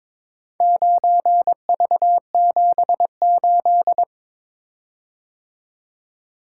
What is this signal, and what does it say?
Morse code, heard as a keyed tone, about 22 wpm, 700 Hz. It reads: 9V78
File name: practice / 22wpm / Four Letter-Number Combinations - 22wpm